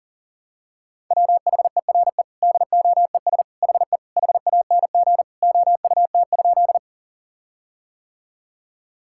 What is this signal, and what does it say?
Morse code, heard as a keyed tone, about 37 wpm, 700 Hz. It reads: WHERE DOES HE HANG OUT?
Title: Sentences from Top 1600 Words 40wpm